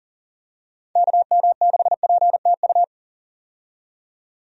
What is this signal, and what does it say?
Morse code, heard as a keyed tone, about 40 wpm, 690 Hz. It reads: KM6PTU
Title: USA Call Signs - 6 Characters - 40wpm